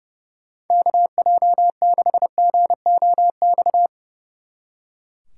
Morse code, heard as a keyed tone, about 36 wpm, 700 Hz. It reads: KJ6GOX
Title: Speed Racing - USA 2x3 Call Signs 30wpm